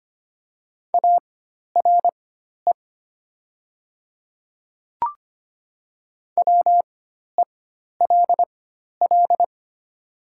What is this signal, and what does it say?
Morse code, heard as a keyed tone, about 25 wpm, spaced as for 12 wpm, 700 Hz. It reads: ARE WELL
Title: 4x Character Spacing - Top 200 Words - 25wpm